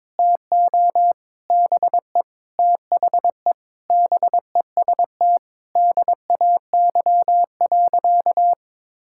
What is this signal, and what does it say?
Morse code, heard as a keyed tone, about 22 wpm, 700 Hz. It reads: TO BE THE BEST DAY.
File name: practice / 22wpm / Sentences with 4-Word Phrases - 22wpm